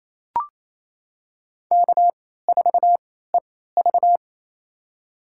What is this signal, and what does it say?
Morse code, heard as a keyed tone, about 28 wpm, spaced as for 16 wpm, 700 Hz. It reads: K4EV
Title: USA Call Signs - 4 Characters - 3x Character Spacing - 28wpm